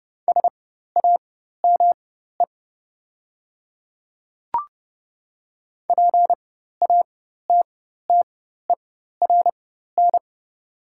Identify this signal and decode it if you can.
Morse code, heard as a keyed tone, about 30 wpm, spaced as for 14 wpm, 690 Hz. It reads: SAME PATTERN
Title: 4x Character Spacing - Top 500 Words - 30wpm